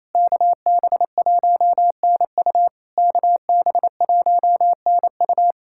Morse code, heard as a keyed tone, about 28 wpm, 700 Hz. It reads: KB1NU KB1NU
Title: SOTA QSO Practice 28wpm